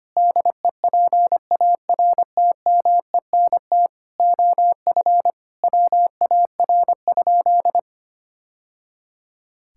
Morse code, heard as a keyed tone, about 25 wpm, 700 Hz. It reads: DEPARTMENT OF WAR?
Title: Sentences from Top 1000 Words - Review Part 5 - 25wpm